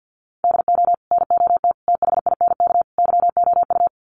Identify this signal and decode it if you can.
Morse code, heard as a keyed tone, about 50 wpm, 700 Hz. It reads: DO NOT THINK YOU
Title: Sets of 4 Words 50wpm